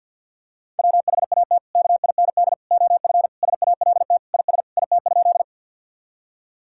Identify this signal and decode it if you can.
Morse code, heard as a keyed tone, about 50 wpm, 700 Hz. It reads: WHAT KIND OF SALT IS IT?
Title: Sentences from Top 1000 Words - Review Part 5 - 50wpm